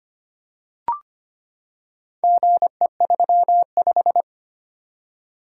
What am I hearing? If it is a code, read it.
Morse code, heard as a keyed tone, about 25 wpm, 700 Hz. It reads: GE35
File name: practice / 25wpm / Four Letter-Number Combinations - 25wpm